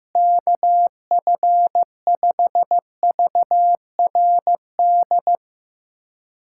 Morse code, heard as a keyed tone, about 15 wpm, 700 Hz. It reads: KF5VRD
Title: K1USN SST QSO Practice - 15wpm